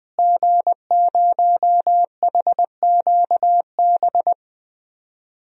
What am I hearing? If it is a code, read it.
Morse code, heard as a keyed tone, about 20 wpm, 700 Hz. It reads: G0HQB
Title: DXCC-100 Call Signs - 5 Characters - Repeated 3x - 20wpm